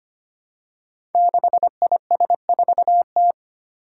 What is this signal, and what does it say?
Morse code, heard as a keyed tone, about 25 wpm, 700 Hz. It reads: 6IS4T
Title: Five Challenging Letter-Number Combinations 25wpm